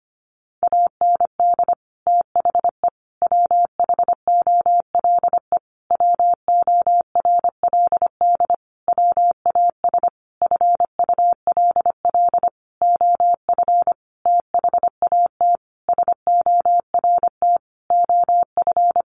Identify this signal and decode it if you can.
Morse code, heard as a keyed tone, about 25 wpm, 700 Hz. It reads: AND THE WHOLE WORLD WAS FULL OF THAT SORT OF
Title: Sentences from Top 1300 Words 25wpm